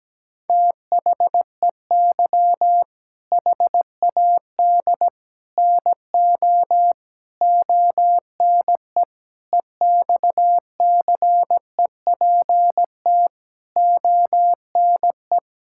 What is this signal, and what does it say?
Morse code, heard as a keyed tone, about 17 wpm, 700 Hz. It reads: THEY HAD NO ONE EXCEPT ONE
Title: Sentences from Top 1600 Words 17wpm